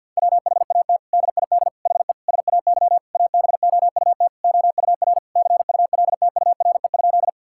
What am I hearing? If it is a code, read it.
Morse code, heard as a keyed tone, about 47 wpm, 700 Hz. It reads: WHAT DID HE SAY ABOUT OUR CULTURE?
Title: Sentences from Top 1000 Words - Review Part 6 - 50wpm